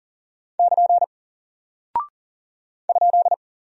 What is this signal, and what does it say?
Morse code, heard as a keyed tone, about 39 wpm, 700 Hz. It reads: <KN> ?
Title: Single Character (Letter:Number:Punctuation:Prosign) 40wpm